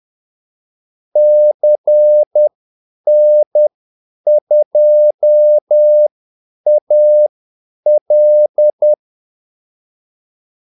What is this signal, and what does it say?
Morse code, heard as a keyed tone, about 10 wpm, spaced as for 8 wpm, 600 Hz. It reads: CN2AL
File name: Callsigns-10-8wpm